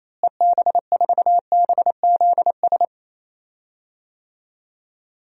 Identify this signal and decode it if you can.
Morse code, heard as a keyed tone, about 27 wpm, 700 Hz. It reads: EB4BZS
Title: DXCC-100 Call Signs - 6 Characters - Repeated 3x - 28wpm